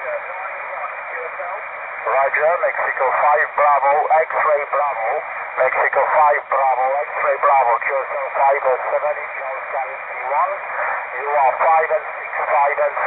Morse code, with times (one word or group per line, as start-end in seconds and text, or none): none